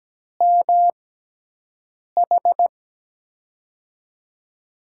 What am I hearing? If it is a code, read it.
Morse code, heard as a keyed tone, about 17 wpm, spaced as for 6 wpm, 700 Hz. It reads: MH
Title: USA Call Signs - 5 Characters - 6x Character Spacing - 17wpm